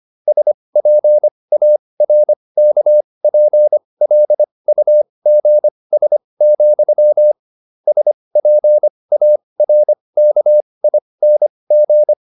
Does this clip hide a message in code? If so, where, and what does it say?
Yes: Morse code from 0.3 to 12.1 s, heard as a keyed tone, about 25 wpm, spaced as for 20 wpm, 600 Hz. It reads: SPARKPLUGS, SPARKING